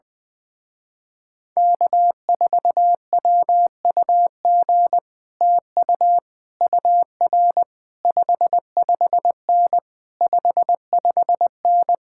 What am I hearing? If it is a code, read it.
Morse code, heard as a keyed tone, about 20 wpm, 700 Hz. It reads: K4WUG TU UR 55N 55N
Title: Speed-Racing Common QSO Phrases Encore - 20wpm